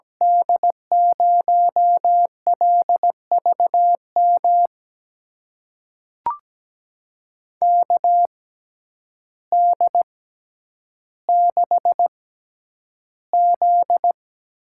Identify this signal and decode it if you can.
Morse code, heard as a keyed tone, about 17 wpm, spaced as for 9 wpm, 700 Hz. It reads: D0LVM KD6Z
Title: USA Call Signs - 6 Characters - 6x Character Spacing - 17wpm